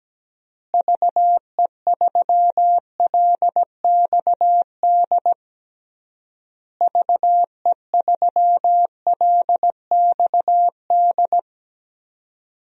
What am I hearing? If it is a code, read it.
Morse code, heard as a keyed tone, about 17 wpm, 700 Hz. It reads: VE3LXD VE3LXD
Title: DXCC-100 Call Signs - 6 Characters - Repeated 3x - 17wpm